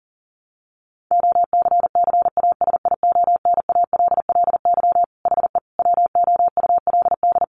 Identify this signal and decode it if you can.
Morse code, heard as a keyed tone, about 40 wpm, 700 Hz. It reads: OCCASIONALLY HE WOULD
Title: Sentences from Top 1300 Words 40wpm